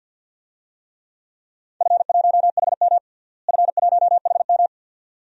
Morse code, heard as a keyed tone, about 52 wpm, 700 Hz. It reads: F1HM F1HM
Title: DXCC-100 Call Signs - 4 Characters - Repeated 2x - 50wpm